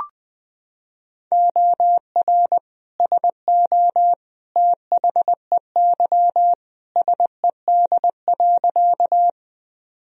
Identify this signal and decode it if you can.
Morse code, heard as a keyed tone, about 20 wpm, 700 Hz. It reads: OR SO THEY SED.
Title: Sentences from Top 100 Words and Abbreviations 20wpm